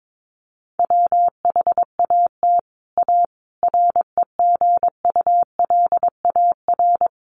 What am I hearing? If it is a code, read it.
Morse code, heard as a keyed tone, about 22 wpm, 700 Hz. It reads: WHAT A REGULAR
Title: Sentences from Top 900 Words 22wpm